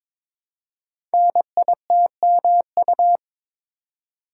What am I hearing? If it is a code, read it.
Morse code, heard as a keyed tone, about 22 wpm, 700 Hz. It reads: NITMU
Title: Code Groups - 5 Letter:Numbers - 22wpm